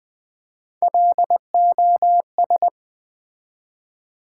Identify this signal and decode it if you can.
Morse code, heard as a keyed tone, about 20 wpm, 700 Hz. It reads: LOS
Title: Build A Word - 1 Letter At A Time - 4-Letter Words - 20wpm